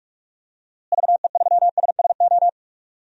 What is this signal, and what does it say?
Morse code, heard as a keyed tone, about 46 wpm, 700 Hz. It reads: VE3SSO
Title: DXCC-100 Call Signs - 6 Characters - Repeated 3x - 45wpm